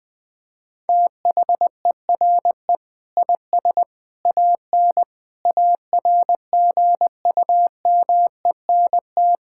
Morse code, heard as a keyed tone, about 20 wpm, 700 Hz. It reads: THERE IS AN ARGUMENT
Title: Sets of 8 Words - 20wpm